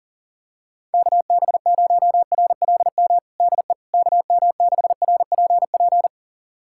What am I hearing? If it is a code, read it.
Morse code, heard as a keyed tone, about 40 wpm, 700 Hz. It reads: KB0RLM DE KM6RPP
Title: USA Callsign DE Callsign - 6 Characters - 40wpm